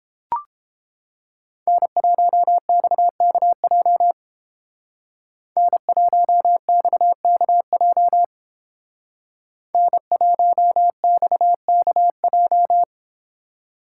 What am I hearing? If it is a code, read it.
Morse code, heard as a keyed tone, about 30 wpm, 700 Hz. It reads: N1XKJ N1XKJ N1XKJ